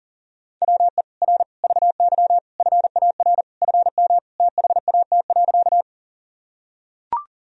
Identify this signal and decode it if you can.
Morse code, heard as a keyed tone, about 40 wpm, 700 Hz. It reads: WE R VY FAR FM THAT.
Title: Sentences from Top 300 Words and Abbreviations 40wpm